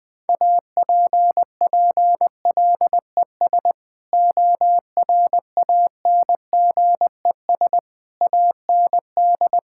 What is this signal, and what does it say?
Morse code, heard as a keyed tone, about 21 wpm, 700 Hz. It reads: APPLES ORANGES AND
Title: Sentences with 6-Word Phrases - 20wpm